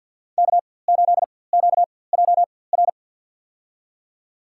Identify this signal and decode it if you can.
Morse code, heard as a keyed tone, about 47 wpm, spaced as for 23 wpm, 700 Hz. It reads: K8QJR
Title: USA Call Signs - 5 Characters - 4x Character Spacing - 50wpm